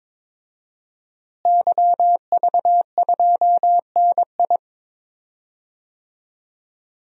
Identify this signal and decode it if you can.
Morse code, heard as a keyed tone, about 22 wpm, 700 Hz. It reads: YV2NI
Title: DXCC-100 Call Signs - 5 Characters - Repeated 2x - 22wpm